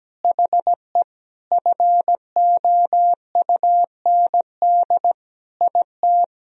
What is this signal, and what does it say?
Morse code, heard as a keyed tone, about 17 wpm, 700 Hz. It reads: HE FOUND IT
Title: Sentences from Top 1000 Words - Review Part 3 - 17wpm